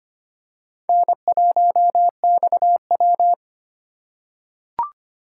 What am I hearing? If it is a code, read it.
Morse code, heard as a keyed tone, about 25 wpm, 700 Hz. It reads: N1XW
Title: USA Call Signs - 4 Characters - 4x Character Spacing - 25wpm